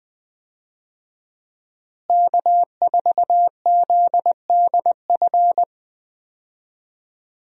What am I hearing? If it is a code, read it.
Morse code, heard as a keyed tone, about 20 wpm, 700 Hz. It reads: K4ZDF
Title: USA Call Signs - 5 Characters - Repeated 3x - 20wpm